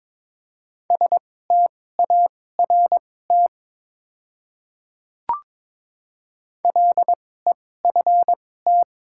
Morse code, heard as a keyed tone, about 22 wpm, spaced as for 16 wpm, 700 Hz. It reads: START LEFT
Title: 2x Character Spacing - Top 300 Words - 22wpm